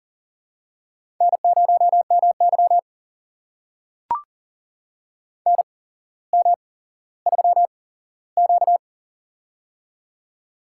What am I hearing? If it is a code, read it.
Morse code, heard as a keyed tone, about 40 wpm, spaced as for 17 wpm, 700 Hz. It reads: N0MY NM3Q
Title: USA Call Signs - 4 Characters - 8x Character Spacing - 40wpm